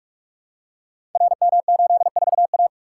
Morse code, heard as a keyed tone, about 44 wpm, 700 Hz. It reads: RM84A
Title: Code Groups - 5 Letter:Numbers - 45wpm